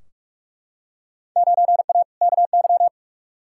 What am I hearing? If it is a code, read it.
Morse code, heard as a keyed tone, about 43 wpm, 700 Hz. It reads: 9A KY
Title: Field Day Exchanges Encore - 45wpm